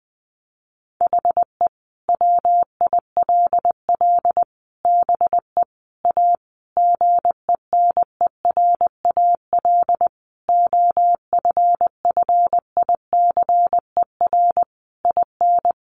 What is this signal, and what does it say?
Morse code, heard as a keyed tone, about 20 wpm, 700 Hz. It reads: HE WILL BE A GENERAL OFFICER IN
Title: Sentences from Top 1200 Words 20wpm